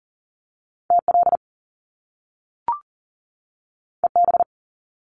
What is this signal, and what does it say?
Morse code, heard as a keyed tone, about 40 wpm, 700 Hz. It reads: TL EB
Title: Top Two Letter Combinations in Words 40wpm